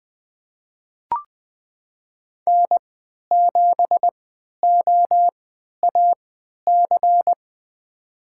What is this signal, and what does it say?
Morse code, heard as a keyed tone, about 20 wpm, spaced as for 11 wpm, 700 Hz. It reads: N7OAC
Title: USA Call Signs - 5 Characters - 3x Character Spacing - 20wpm